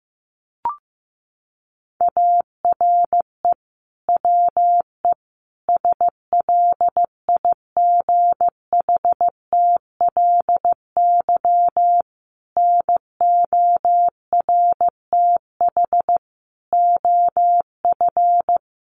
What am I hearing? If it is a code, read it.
Morse code, heard as a keyed tone, about 15 wpm, 700 Hz. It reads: ARE WE SLIGHTLY NORTH OF
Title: Sentences from Top 1300 Words 15wpm